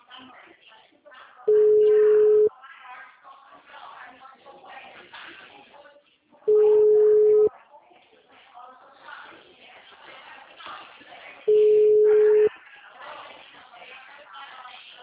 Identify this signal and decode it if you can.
Ringback tone (Europe, 425 Hz), with speech.